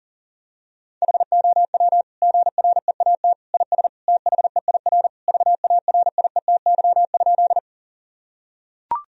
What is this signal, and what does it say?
Morse code, heard as a keyed tone, about 37 wpm, 700 Hz. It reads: HOW GREAT IS THEIR VARIETY?
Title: Sentences from Top 1000 Words - Review Part 4 - 40wpm